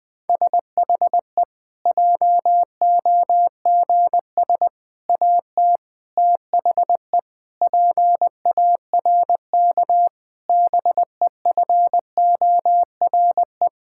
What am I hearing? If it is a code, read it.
Morse code, heard as a keyed tone, about 20 wpm, 700 Hz. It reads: SHE JOGS AT THE PARK BEFORE
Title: Difficult Sentences with Hints - 20wpm